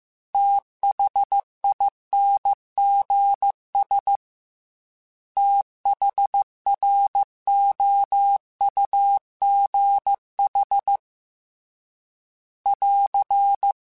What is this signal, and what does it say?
Morse code, heard as a keyed tone, about 15 wpm, 800 Hz. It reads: THINGS THROUGH <AR>